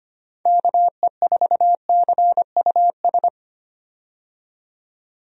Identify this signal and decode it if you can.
Morse code, heard as a keyed tone, about 25 wpm, 700 Hz. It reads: KE4CUS
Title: USA Call Signs - 6 Characters - 25wpm